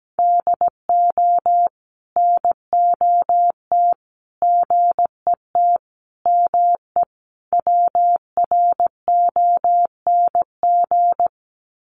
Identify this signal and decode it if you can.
Morse code, heard as a keyed tone, about 17 wpm, 700 Hz. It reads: DO NOT GET ME WRONG
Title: Sets of 5 Words Encore 17wpm